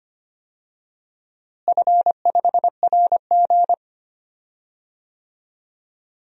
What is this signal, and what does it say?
Morse code, heard as a keyed tone, about 25 wpm, 700 Hz. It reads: F5RG
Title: DXCC-100 Call Signs - 4 Characters - Repeated 3x - 25wpm